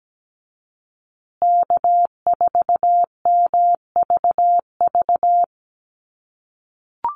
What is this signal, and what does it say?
Morse code, heard as a keyed tone, about 17 wpm, 700 Hz. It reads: K4MVV
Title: USA Call Signs Repeated 3x 17wpm